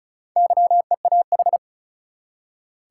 Morse code, heard as a keyed tone, about 36 wpm, 700 Hz. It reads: YEAH
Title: Four Letter Words - 35wpm